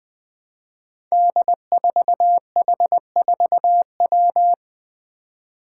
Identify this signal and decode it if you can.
Morse code, heard as a keyed tone, about 20 wpm, 700 Hz. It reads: D4H4W
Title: Five Challenging Letter-Number Combinations 20wpm